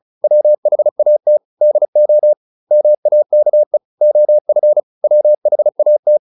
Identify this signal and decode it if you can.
Morse code, heard as a keyed tone, about 34 wpm, 600 Hz. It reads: WHAT DO MAKE OF WHAT